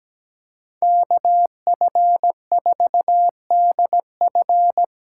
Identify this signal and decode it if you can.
Morse code, heard as a keyed tone, about 17 wpm, 700 Hz. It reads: KF4DF
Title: USA Call Signs - 6 Characters - 8x Character Spacing - 17wpm